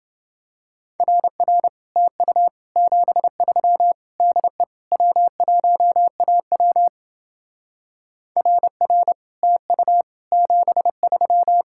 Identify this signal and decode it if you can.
Morse code, heard as a keyed tone, about 27 wpm, 700 Hz. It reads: RR TU 73 DE W1AW RR TU 73